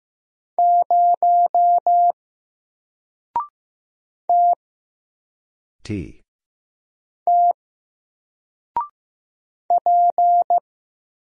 Morse code, heard as a keyed tone, about 15 wpm, 700 Hz. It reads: 0 T T P
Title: Single Letter-Number 15wpm